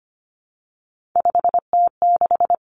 Morse code, heard as a keyed tone, about 25 wpm, 700 Hz. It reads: 5T6
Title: Three Challenging Letter-Number Combinations 25wpm